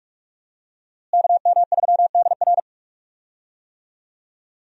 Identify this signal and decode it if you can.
Morse code, heard as a keyed tone, about 46 wpm, 700 Hz. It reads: KM3DR
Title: USA Call Signs - 5 Characters - Repeated 3x - 45wpm